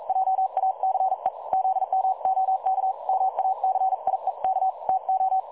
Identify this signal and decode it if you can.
Morse code, heard as a keyed tone, about 42 wpm, 760 Hz. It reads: JACE ZMOK MACIEK TO
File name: dlazaawansowanych / nagraniazpasma / websdr recording 3565.0kHz